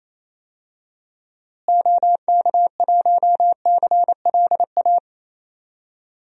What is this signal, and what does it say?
Morse code, heard as a keyed tone, about 28 wpm, 700 Hz. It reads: OK1CLA
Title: DXCC-100 Call Signs - 6 Characters - Repeated 3x - 28wpm